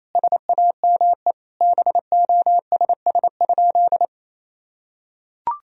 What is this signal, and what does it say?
Morse code, heard as a keyed tone, about 28 wpm, 700 Hz. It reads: SAME BOSS?
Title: Sentences from Top 1000 Words - Review Part 4 - 28wpm